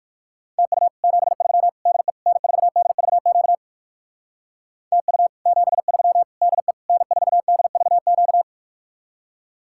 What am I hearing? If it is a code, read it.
Morse code, heard as a keyed tone, about 57 wpm, 700 Hz. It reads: TU 73 DE N4DVQ TU 73 DE N4DVQ